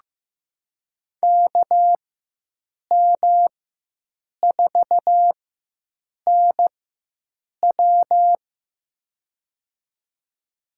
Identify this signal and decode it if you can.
Morse code, heard as a keyed tone, about 15 wpm, spaced as for 7 wpm, 700 Hz. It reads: KM4NW
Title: USA Call Signs - 4x Character Spacing - 15wpm